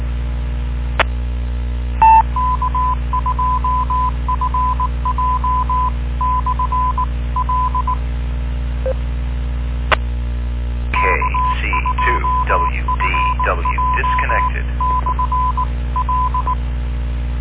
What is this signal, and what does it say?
Morse code, heard as a keyed tone, about 18 wpm, 1010 Hz, with noise. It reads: E K2FJ/L E K2FJ/L